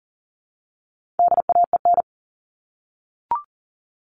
Morse code, heard as a keyed tone, about 40 wpm, 700 Hz. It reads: DAEN
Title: All Cut Numbers - 4-Digit 40wpm